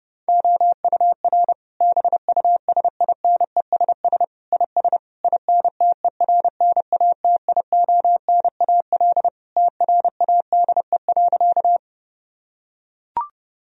Morse code, heard as a keyed tone, about 30 wpm, 700 Hz. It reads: OUR BUSINESS IS INTERNATIONAL TRADE.
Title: Sentences from Top 1000 Words - Review Part 4 - 30wpm